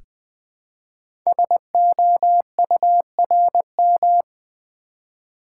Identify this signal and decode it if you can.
Morse code, heard as a keyed tone, about 20 wpm, 700 Hz. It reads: SOURM
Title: Code Groups - 5 Letters - 20wpm